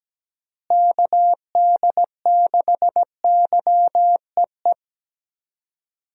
Morse code, heard as a keyed tone, about 17 wpm, 700 Hz. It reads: KD6YEE
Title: USA Call Signs - 6 Characters - Repeated 3x - 17wpm